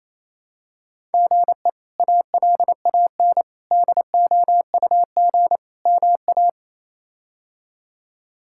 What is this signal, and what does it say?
Morse code, heard as a keyed tone, about 28 wpm, 700 Hz. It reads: GE ALAN DOUG MA
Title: K1USN SST QSO Practice - 28wpm